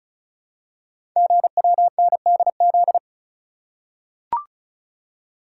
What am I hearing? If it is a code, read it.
Morse code, heard as a keyed tone, about 35 wpm, 700 Hz. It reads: GWNDZ
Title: Code Groups - 5 Letters - 35wpm